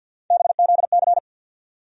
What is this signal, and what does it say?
Morse code, heard as a keyed tone, about 50 wpm, 700 Hz. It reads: BZC
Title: USA Call Sign Postfixes - 3 Letters 50wpm